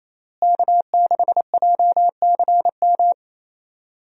Morse code, heard as a keyed tone, about 28 wpm, 700 Hz. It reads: K6JCM
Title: USA Call Signs - 5 Characters - Repeated 2x - 28wpm